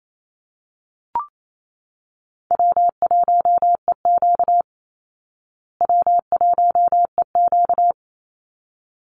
Morse code, heard as a keyed tone, about 28 wpm, 700 Hz. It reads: W1EQ W1EQ
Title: Easy Call Signs Repeated 2x - 28wpm